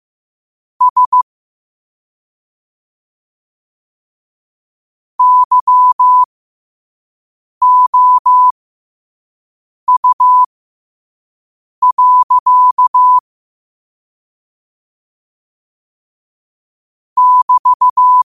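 Morse code, heard as a keyed tone, about 15 wpm, spaced as for 5 wpm, 1000 Hz. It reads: S YOU. <BT>
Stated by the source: Morse code at 5 wpm.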